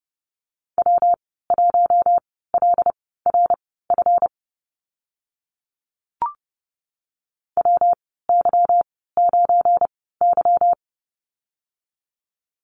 Morse code, heard as a keyed tone, about 32 wpm, spaced as for 17 wpm, 700 Hz. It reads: W1LRF WY9Y
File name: Lesson 37g - Letter x - Call Signs 30wpm